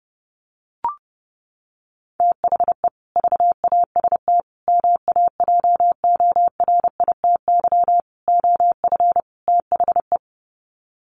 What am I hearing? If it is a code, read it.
Morse code, heard as a keyed tone, about 30 wpm, 700 Hz. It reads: THE VAST MAJORITY OF THE
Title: Sets of 5 Words Encore 30wpm